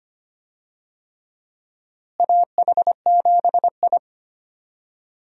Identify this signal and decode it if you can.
Morse code, heard as a keyed tone, about 25 wpm, 700 Hz. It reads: AH7I
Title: USA Call Signs - 4 Characters - Repeated 3x - 25wpm